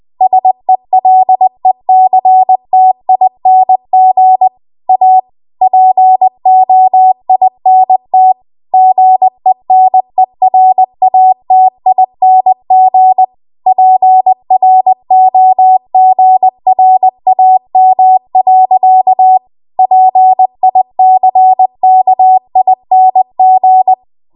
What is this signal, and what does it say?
Morse code, heard as a keyed tone, about 20 wpm, 750 Hz. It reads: SELECTING A POINT GENERATING PROGRAM. PICKING
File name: Archive / 20 WPM / 20WPM